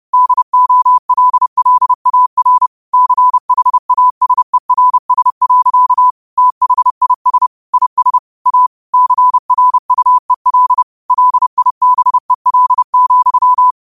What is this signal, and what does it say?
Morse code, heard as a keyed tone, about 30 wpm, 1000 Hz. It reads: DOLLAR CHASERS. THIS IS A CRUEL LIBEL,
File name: qotdfiles / 30wpm